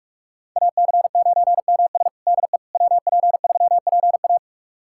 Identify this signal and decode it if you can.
Morse code, heard as a keyed tone, about 41 wpm, 700 Hz. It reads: AC9MS DE WP3PA
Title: USA Callsign DE Callsign - 5 Characters - 45wpm